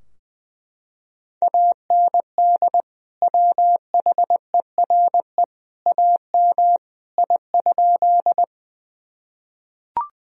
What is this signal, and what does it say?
Morse code, heard as a keyed tone, about 20 wpm, 700 Hz. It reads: AND WHERE AM I?
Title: Sentences from Top 1000 Words - Review Part 2 - 20wpm